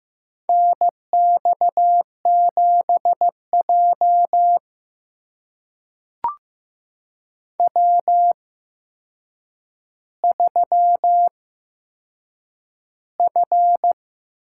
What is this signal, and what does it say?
Morse code, heard as a keyed tone, about 15 wpm, spaced as for 7 wpm, 700 Hz. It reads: NX7J W3F
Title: USA Call Signs - 4 Characters - 8x Character Spacing - 15wpm